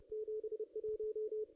Morse code, heard as a keyed tone, about 29 wpm, 440 Hz. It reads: 71
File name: contests / ss-cw-k5tr-op-n5rz -0400